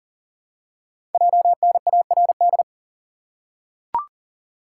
Morse code, heard as a keyed tone, about 41 wpm, 690 Hz. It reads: JNARD
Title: Code Groups - 5 Letters - 40wpm